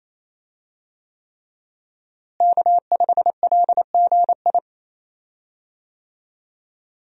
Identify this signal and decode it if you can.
Morse code, heard as a keyed tone, about 28 wpm, 700 Hz. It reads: K5LGI
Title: USA Call Signs - 5 Characters - Repeated 3x - 28wpm